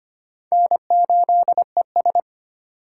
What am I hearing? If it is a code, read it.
Morse code, heard as a keyed tone, about 25 wpm, 700 Hz. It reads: N8ES
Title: USA Call Signs - 4 Characters - 3x Character Spacing - 25wpm